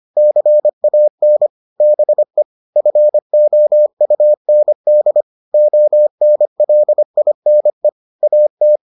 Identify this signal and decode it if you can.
Morse code, heard as a keyed tone, about 25 wpm, 600 Hz. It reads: CAN BE FOUND ONLINE AT